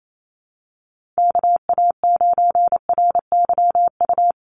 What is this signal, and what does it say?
Morse code, heard as a keyed tone, about 28 wpm, 700 Hz. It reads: KA9RYU
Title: USA Call Signs Repeated 2x 28wpm